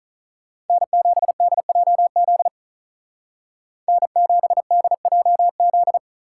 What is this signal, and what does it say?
Morse code, heard as a keyed tone, about 46 wpm, 700 Hz. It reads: N7DJZ N7DJZ